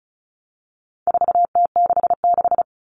Morse code, heard as a keyed tone, about 34 wpm, 700 Hz. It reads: 4T66
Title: All Cut Numbers - 4-Digit 35wpm